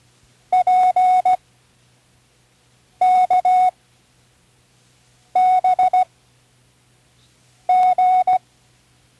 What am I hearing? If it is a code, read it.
Morse code, heard as a keyed tone, about 16 wpm, 740 Hz. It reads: P K B G
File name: CWResources / Learning Morse Code (from Tune in the World) / 13 random